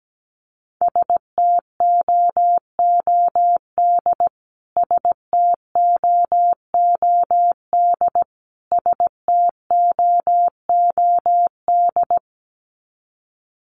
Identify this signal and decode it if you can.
Morse code, heard as a keyed tone, about 17 wpm, 700 Hz. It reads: STOOD STOOD STOOD
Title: Top 500 Words Repeated 17wpm